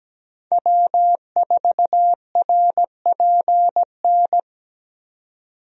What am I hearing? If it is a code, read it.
Morse code, heard as a keyed tone, about 17 wpm, 700 Hz. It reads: W4RPN
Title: USA Call Signs - 5 Characters - 17wpm